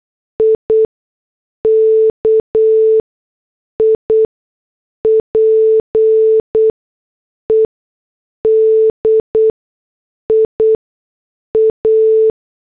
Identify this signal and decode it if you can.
Morse code, heard as a keyed tone, about 8 wpm, 440 Hz. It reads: I K I P E D I A
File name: morse / headlines